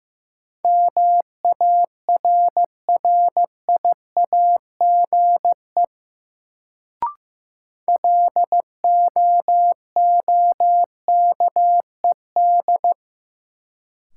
Morse code, heard as a keyed tone, about 15 wpm, 700 Hz. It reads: MARRIAGE LOOKED
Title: Sentences from Top 800 Words - New Word Drill - 15wpm